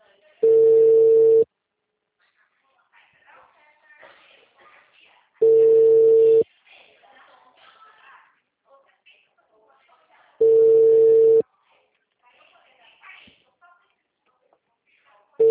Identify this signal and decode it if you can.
Ringback tone (China, 450 Hz), with speech.